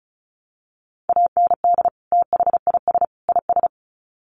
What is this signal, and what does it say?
Morse code, heard as a keyed tone, about 34 wpm, 700 Hz. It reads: AND THIS IS
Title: Sets of 3 Words 35wpm